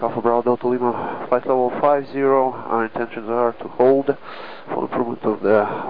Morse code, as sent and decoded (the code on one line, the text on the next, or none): none